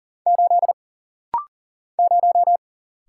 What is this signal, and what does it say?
Morse code, heard as a keyed tone, about 40 wpm, 700 Hz. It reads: 8 0
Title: Single Letter-Number - Rapid-Fire 40wpm